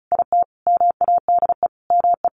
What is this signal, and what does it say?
Morse code, heard as a keyed tone, about 34 wpm, 700 Hz. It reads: IT MADE ME
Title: Sentences from Top 600 Words 35wpm